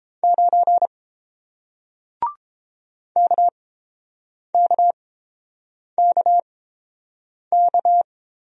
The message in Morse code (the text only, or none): 9 KKKK